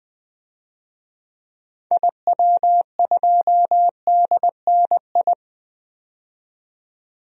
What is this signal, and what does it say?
Morse code, heard as a keyed tone, about 21 wpm, 700 Hz. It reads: IW2DNI
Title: DXCC-100 Call Signs - 6 Characters - 20wpm